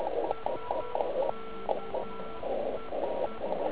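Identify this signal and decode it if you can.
Morse code, heard as a keyed tone, about 10 wpm, 1340 Hz, with radio noise. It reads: SK EE
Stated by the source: Slow (QRS) CW test by beginner OK1IFW during beginner Morse skeds on 3559.5 kHz (80 m).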